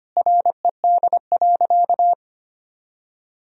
Morse code, heard as a keyed tone, about 25 wpm, 700 Hz. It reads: RED.
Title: Sentences from Top 1000 Words - Review Part 2 - 25wpm